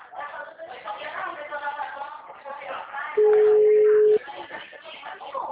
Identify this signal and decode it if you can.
Ringback tone (Europe, 425 Hz), with speech.